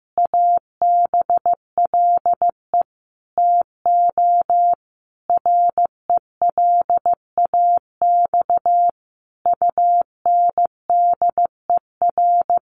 Morse code, heard as a keyed tone, about 15 wpm, 700 Hz. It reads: ABLE TO RELAX UNDER
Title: Sentences from Top 1400 Words 15wpm